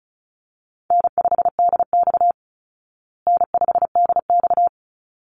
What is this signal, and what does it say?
Morse code, heard as a keyed tone, about 34 wpm, 700 Hz. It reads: N5DX N5DX
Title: Easy Call Signs Repeated 2x - 35wpm